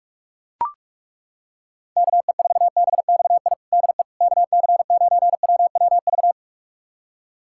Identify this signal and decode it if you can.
Morse code, heard as a keyed tone, about 43 wpm, 700 Hz. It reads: KE4BXI DE KC9WWV
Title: USA Callsign DE Callsign 45wpm